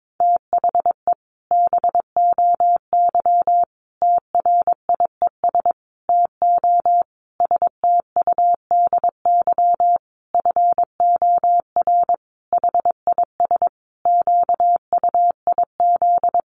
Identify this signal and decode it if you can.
Morse code, heard as a keyed tone, about 22 wpm, 700 Hz. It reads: THE BOY TRIES TO STUDY FOR HIS QUIZ
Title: Additional Easy Phrases 22wpm